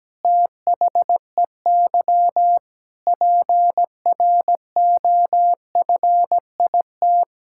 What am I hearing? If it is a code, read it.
Morse code, heard as a keyed tone, about 17 wpm, 700 Hz. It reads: THEY PROFIT
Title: Sentences from Top 1000 Words - Review Part 5 - 17wpm